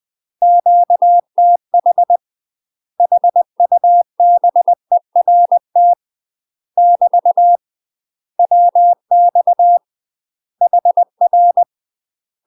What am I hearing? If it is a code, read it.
Morse code, heard as a keyed tone, about 20 wpm, 700 Hz. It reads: QTH HUBERT <BT> WX HR
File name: UK MASTER QSOs TRAINING / 20WPM / UK MASTER QSOs 0053